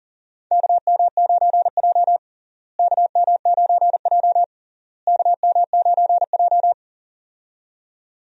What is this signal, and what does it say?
Morse code, heard as a keyed tone, about 40 wpm, 700 Hz. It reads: KM9J KM9J KM9J
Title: USA Call Signs - 4 Characters - Repeated 3x - 40wpm